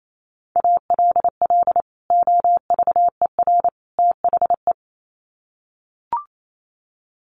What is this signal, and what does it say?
Morse code, heard as a keyed tone, about 28 wpm, 700 Hz. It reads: ALL OVER THE
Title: Sets of 3 Words 28wpm